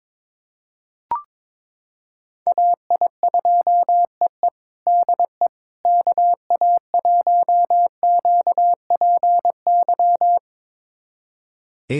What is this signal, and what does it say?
Morse code, heard as a keyed tone, about 22 wpm, 700 Hz. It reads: AI2EE DE KA1QPY
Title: USA Callsign DE Callsign 22wpm